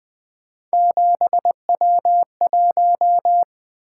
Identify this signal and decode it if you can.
Morse code, heard as a keyed tone, about 20 wpm, 700 Hz. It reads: 7W1
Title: Three Letter-Number Combinations - 20wpm